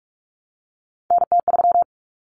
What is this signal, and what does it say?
Morse code, heard as a keyed tone, about 44 wpm, 700 Hz. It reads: NT3
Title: Common Cut Numbers - 3-Digit 45wpm